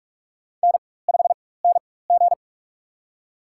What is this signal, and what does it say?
Morse code, heard as a keyed tone, about 47 wpm, spaced as for 21 wpm, 700 Hz. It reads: N5NG
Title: USA Call Signs - 4x Character Spacing - 45wpm